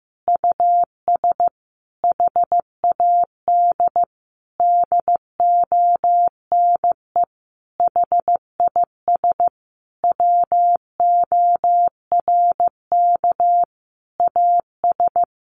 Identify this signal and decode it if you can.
Morse code, heard as a keyed tone, about 14 wpm, 700 Hz. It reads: US HAD DONE HIS WORK AS
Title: Sentences from Top 500 Words 15wpm